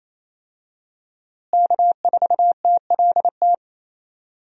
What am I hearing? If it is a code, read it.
Morse code, heard as a keyed tone, about 28 wpm, 700 Hz. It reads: K4TLT
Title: USA Call Signs - 5 Characters - Repeated 3x - 28wpm